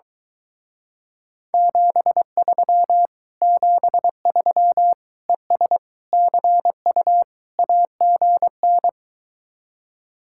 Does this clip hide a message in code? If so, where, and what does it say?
Yes: Morse code from 1.5 to 8.9 s, heard as a keyed tone, about 23 wpm, 700 Hz. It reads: 73 73 ES CU AGN